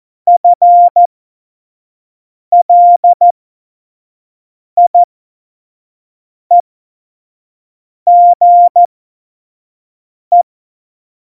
Morse code, heard as a keyed tone, about 14 wpm, spaced as for 5 wpm, 700 Hz. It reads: FLIEGE
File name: Morsekurs / Witze / Witz 26. November 2019, 25 F70